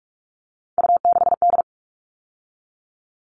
Morse code, heard as a keyed tone, about 44 wpm, 700 Hz. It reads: U6D
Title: All Cut Numbers - 3-Digit 45wpm